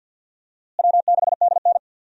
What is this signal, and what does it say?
Morse code, heard as a keyed tone, about 46 wpm, 700 Hz. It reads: W6DN
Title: USA Call Signs - 4 Characters - Repeated 2x - 50wpm